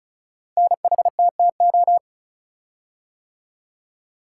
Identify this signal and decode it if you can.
Morse code, heard as a keyed tone, about 36 wpm, 700 Hz. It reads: NHTTO
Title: Code Groups - 5 Letters - 35wpm